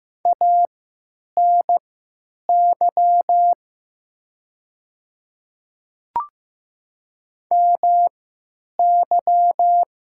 Morse code, heard as a keyed tone, about 15 wpm, spaced as for 9 wpm, 700 Hz. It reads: ANY MY
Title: 3x Character Spacing - Top 200 Words - 15wpm